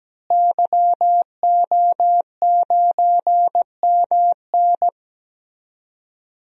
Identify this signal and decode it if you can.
Morse code, heard as a keyed tone, about 17 wpm, 700 Hz. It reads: YO9MN
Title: DXCC-100 Call Signs - 5 Characters - Repeated 3x - 17wpm